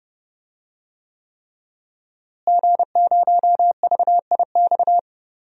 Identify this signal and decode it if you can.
Morse code, heard as a keyed tone, about 31 wpm, 700 Hz. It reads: G0VIX
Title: DXCC-100 Call Signs - 5 Characters - Repeated 3x - 30wpm